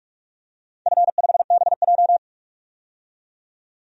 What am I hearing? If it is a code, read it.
Morse code, heard as a keyed tone, about 46 wpm, 700 Hz. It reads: F5BJ
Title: DXCC-100 Call Signs - 4 Characters - 45wpm